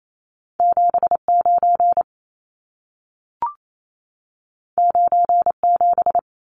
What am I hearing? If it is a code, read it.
Morse code, heard as a keyed tone, about 28 wpm, 700 Hz. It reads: 79 97
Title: Numbers - 2-Digit 28wpm